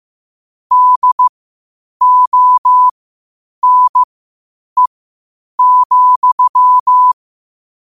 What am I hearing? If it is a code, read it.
Morse code, heard as a keyed tone, about 15 wpm, spaced as for 8 wpm, 1000 Hz. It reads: DONE,
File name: qotdfiles / 8wpm